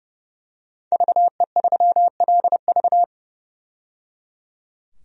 Morse code, heard as a keyed tone, about 30 wpm, 700 Hz. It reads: VE3LV
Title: DXCC-100 Call Signs - 5 Characters - 30wpm